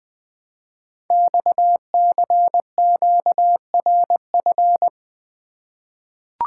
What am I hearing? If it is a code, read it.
Morse code, heard as a keyed tone, about 20 wpm, 700 Hz. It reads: XCQRF
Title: Code Groups - 5 Letters - 20wpm